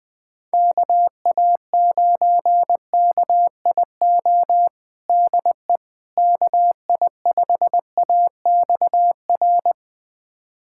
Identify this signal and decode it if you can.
Morse code, heard as a keyed tone, about 20 wpm, 700 Hz. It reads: KA9KIO DE KI5AXR
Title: USA Callsign DE Callsign - 6 Characters - 20wpm